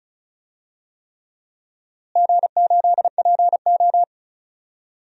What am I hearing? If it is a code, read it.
Morse code, heard as a keyed tone, about 36 wpm, 700 Hz. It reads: G8PO
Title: DXCC-100 Call Signs - 4 Characters - Repeated 3x - 35wpm